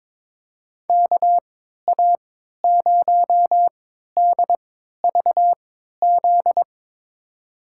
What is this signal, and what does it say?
Morse code, heard as a keyed tone, about 22 wpm, spaced as for 12 wpm, 700 Hz. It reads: KA0DVZ
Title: USA Call Signs - 6 Characters - 3x Character Spacing - 22wpm